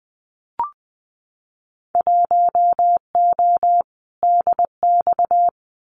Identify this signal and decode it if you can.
Morse code, heard as a keyed tone, about 20 wpm, 700 Hz. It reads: 1O DX
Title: Winter Field Day Exchanges 20wpm